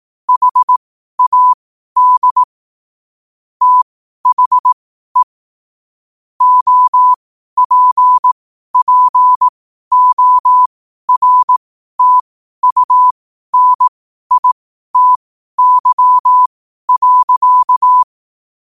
Morse code, heard as a keyed tone, about 18 wpm, spaced as for 12 wpm, 1000 Hz. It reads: HAD THE OPPORTUNITY.
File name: qotdfiles / 12wpm